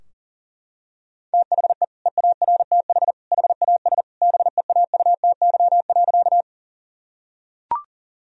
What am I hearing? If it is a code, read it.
Morse code, heard as a keyed tone, about 40 wpm, 700 Hz. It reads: THE EARTH HAS BEAUTY.
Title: Sentences from Top 1000 Words - Review Part 2 - 40wpm